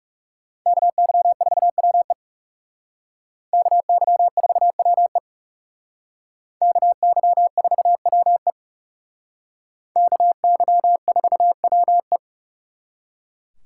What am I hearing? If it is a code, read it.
Morse code, heard as a keyed tone, about 36 wpm, 700 Hz. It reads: KY4WE KY4WE KY4WE KY4WE
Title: Speed Racing - USA Call Signs 30wpm